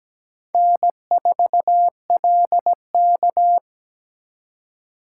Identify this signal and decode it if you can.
Morse code, heard as a keyed tone, about 20 wpm, 700 Hz. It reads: N4LK
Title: Speed Racing - USA 1x2 Call Signs 17wpm